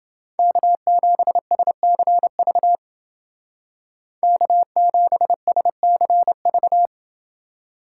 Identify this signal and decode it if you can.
Morse code, heard as a keyed tone, about 28 wpm, 700 Hz. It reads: K7SCV K7SCV